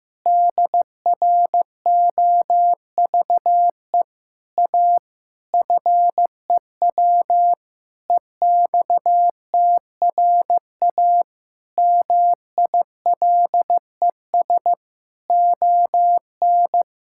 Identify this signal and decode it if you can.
Morse code, heard as a keyed tone, about 15 wpm, 700 Hz. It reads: DROVE A FEW EXTRA MILES ON
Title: Pangram Challenge v2 - 15wpm